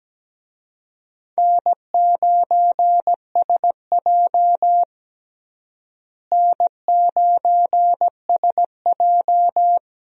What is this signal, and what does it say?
Morse code, heard as a keyed tone, about 17 wpm, 700 Hz. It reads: N9SJ N9SJ
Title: USA Call Signs - 4 Characters - Repeated 3x - 17wpm